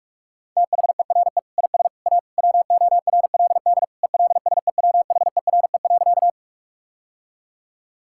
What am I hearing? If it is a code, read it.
Morse code, heard as a keyed tone, about 47 wpm, 700 Hz. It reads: THERE IS A WORLD ELSEWHERE.
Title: Sentences from Top 1600 Words 45wpm